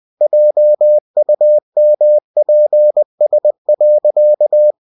Morse code, heard as a keyed tone, about 20 wpm, 600 Hz. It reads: JUMPS.
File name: cw-books / winnie-the-pooh / winnie-20wpm-page-0071